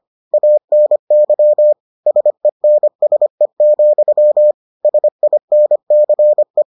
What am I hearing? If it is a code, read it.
Morse code, heard as a keyed tone, about 25 wpm, 600 Hz. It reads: ANY SENSE, SINCE